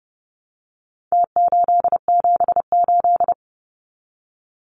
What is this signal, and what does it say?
Morse code, heard as a keyed tone, about 30 wpm, 700 Hz. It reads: T878
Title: Numbers - Contest Style - 4-Digit 30wpm